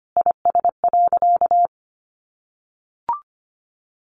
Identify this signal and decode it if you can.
Morse code, heard as a keyed tone, about 25 wpm, 700 Hz. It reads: IS.
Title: Sentences from Top 1400 Words 25wpm